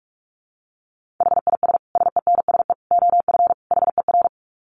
Morse code, heard as a keyed tone, about 44 wpm, 700 Hz. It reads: HIS SENSE OF HER
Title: Dit Challenge 45wpm